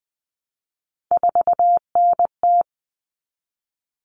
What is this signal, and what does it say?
Morse code, heard as a keyed tone, about 20 wpm, 700 Hz. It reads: 4NT
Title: Common Cut Numbers - 3-Digit 20wpm